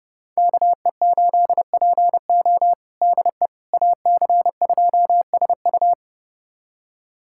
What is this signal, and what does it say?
Morse code, heard as a keyed tone, about 32 wpm, 700 Hz. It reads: KE8PO DE AC2SU
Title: USA Callsign DE Callsign - 5 Characters - 30wpm